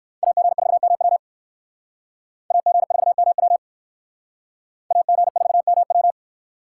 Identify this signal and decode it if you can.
Morse code, heard as a keyed tone, about 62 wpm, 700 Hz. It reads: AG4MW AG4MW AG4MW